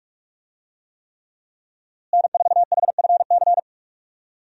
Morse code, heard as a keyed tone, about 44 wpm, 700 Hz. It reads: N4HFC
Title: USA Call Signs - 5 Characters - Repeated 3x - 45wpm